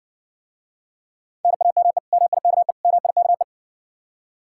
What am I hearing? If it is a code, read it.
Morse code, heard as a keyed tone, about 60 wpm, 700 Hz. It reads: NAME MIKE MIKE